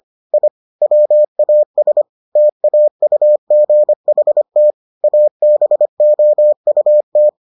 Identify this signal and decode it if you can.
Morse code, heard as a keyed tone, about 25 wpm, 600 Hz. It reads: I WAS TAUGHT ABOUT